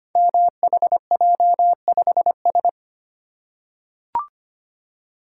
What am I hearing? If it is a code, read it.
Morse code, heard as a keyed tone, about 25 wpm, 700 Hz. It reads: MHJ5S
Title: Code Groups - 5 Letter:Numbers - 25wpm